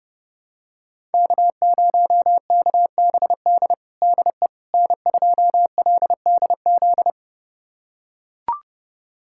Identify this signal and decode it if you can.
Morse code, heard as a keyed tone, about 30 wpm, 700 Hz. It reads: K0KBD DE N2LDZ
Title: USA Callsign DE Callsign - 5 Characters - 30wpm